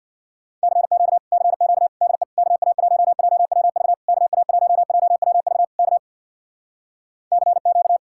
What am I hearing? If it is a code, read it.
Morse code, heard as a keyed tone, about 65 wpm, 700 Hz. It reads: CQ CQ DE KA1JWV KA1JWV K CQ